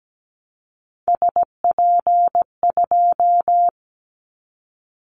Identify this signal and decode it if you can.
Morse code, heard as a keyed tone, about 17 wpm, 700 Hz. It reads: SP2
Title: Callsign Prefixes for Top 25 DXCC - 17wpm